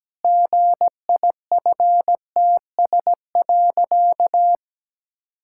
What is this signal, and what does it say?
Morse code, heard as a keyed tone, about 17 wpm, 700 Hz. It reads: GIFTS.
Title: Pangram Challenge v2 - 17wpm